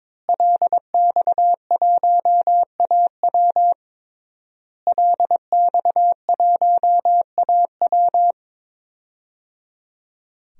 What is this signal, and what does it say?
Morse code, heard as a keyed tone, about 22 wpm, 700 Hz. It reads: LX1AW LX1AW
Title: DXCC-100 Call Signs - 5 Characters - Repeated 3x - 22wpm